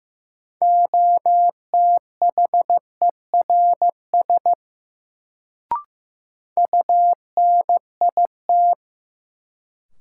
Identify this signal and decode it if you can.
Morse code, heard as a keyed tone, about 15 wpm, 700 Hz. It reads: OTHERS UNIT
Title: Sentences from Top 1000 Words - Review Part 2 - New Word Drill - 15wpm